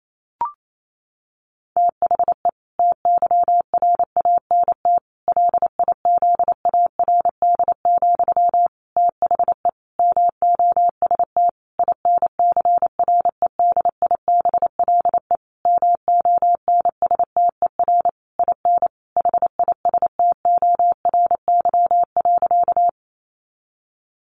Morse code, heard as a keyed tone, about 28 wpm, 700 Hz. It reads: THE TYRANT LIZARD, THE MOST INCREDIBLE MONSTER IN HISTORY.
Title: A Sound Of Thunder 28wpm